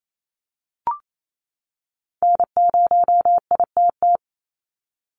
Morse code, heard as a keyed tone, about 28 wpm, 700 Hz. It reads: N0ITT
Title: USA Call Signs 28wpm